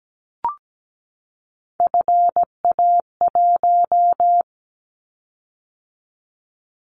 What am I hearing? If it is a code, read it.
Morse code, heard as a keyed tone, about 17 wpm, 700 Hz. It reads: FA1
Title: Callsign Prefixes for Top 50 DXCC - 17wpm